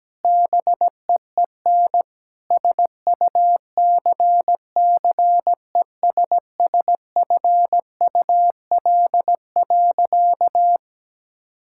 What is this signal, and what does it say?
Morse code, heard as a keyed tone, about 17 wpm, 700 Hz. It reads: BEEN SUCCESSFUL.
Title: Sentences from Top 1600 Words 17wpm